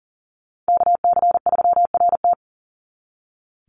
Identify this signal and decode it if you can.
Morse code, heard as a keyed tone, about 40 wpm, 700 Hz. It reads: KC3RT
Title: NA SOTA Chaser Callsigns 40wpm